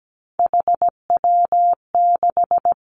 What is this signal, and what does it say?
Morse code, heard as a keyed tone, about 17 wpm, 700 Hz. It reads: HW6
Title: Three Challenging Letter-Number Combinations 17wpm